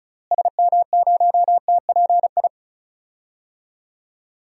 Morse code, heard as a keyed tone, about 36 wpm, 700 Hz. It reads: SM0TPI
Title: DXCC-100 Call Signs - 6 Characters - Repeated 2x - 35wpm